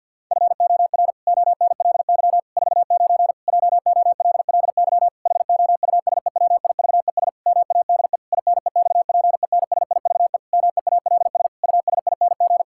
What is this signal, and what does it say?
Morse code, heard as a keyed tone, about 50 wpm, 700 Hz. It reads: FOR ONLY 49 JOLLY HOUSEWIVES MADE INEXPENSIVE MEALS USING
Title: Pangram Challenge v2 - 50wpm